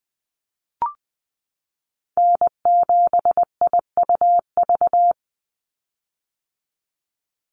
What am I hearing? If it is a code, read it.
Morse code, heard as a keyed tone, about 20 wpm, 700 Hz. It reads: N7IUV
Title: USA Call Signs 20wpm